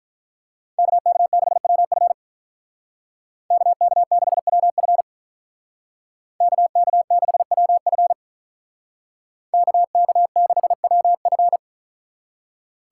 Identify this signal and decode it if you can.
Morse code, heard as a keyed tone, about 44 wpm, 700 Hz. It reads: KK6WF KK6WF KK6WF KK6WF